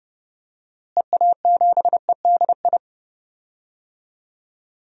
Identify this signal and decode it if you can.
Morse code, heard as a keyed tone, about 31 wpm, 700 Hz. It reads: EA7EDI
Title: DXCC-100 Call Signs - 6 Characters - Repeated 2x - 30wpm